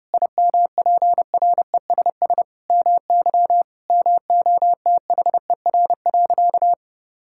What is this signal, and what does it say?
Morse code, heard as a keyed tone, about 32 wpm, 700 Hz. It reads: IMPRESS MY MOTHER.
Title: Sentences from Top 1600 Words 30wpm